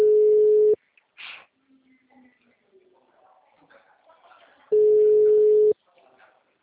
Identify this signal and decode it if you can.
Ringback tone (Europe, 425 Hz), with speech.